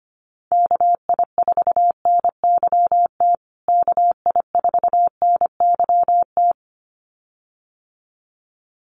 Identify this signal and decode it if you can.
Morse code, heard as a keyed tone, about 25 wpm, 700 Hz. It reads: KI4NYT KI4NYT
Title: USA Call Signs Repeated 2x 25wpm